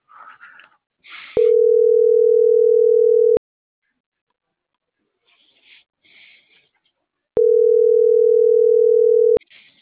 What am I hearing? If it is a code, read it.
Ringback tone (North America), 440 + 480 Hz.